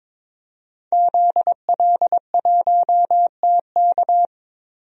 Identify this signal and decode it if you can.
Morse code, heard as a keyed tone, about 22 wpm, 700 Hz. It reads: ZL1TK
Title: DXCC-100 Call Signs - 5 Characters - 22wpm